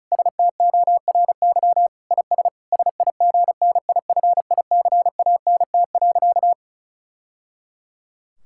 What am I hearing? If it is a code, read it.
Morse code, heard as a keyed tone, about 36 wpm, 700 Hz. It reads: STORY IS SIGNIFICANT.
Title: Sentences from Top 1000 Words - Review Part 6 - 35wpm